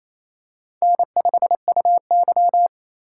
Morse code, heard as a keyed tone, about 28 wpm, 700 Hz. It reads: N5UY
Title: USA Call Signs - 4 Characters - 3x Character Spacing - 28wpm